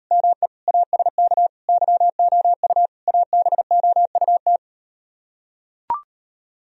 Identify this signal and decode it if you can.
Morse code, heard as a keyed tone, about 38 wpm, 700 Hz. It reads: ME ASK YOU ABOUT